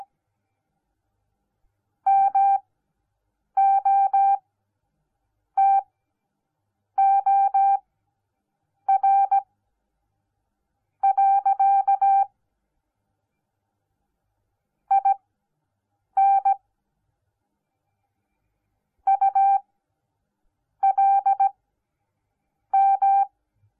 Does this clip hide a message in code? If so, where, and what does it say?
Yes: Morse code from 2.1 to 23.2 s, heard as a keyed tone, about 17 wpm, spaced as for 6 wpm, 790 Hz. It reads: MOTOR. IN ULM